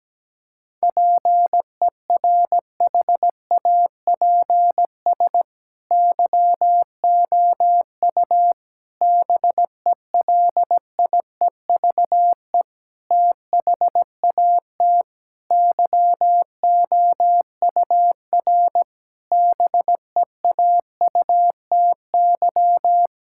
Morse code, heard as a keyed tone, about 17 wpm, 700 Hz. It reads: PERHAPS YOU BELIEVE THAT YOUR BEAUTY
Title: Sentences from Top 1600 Words 17wpm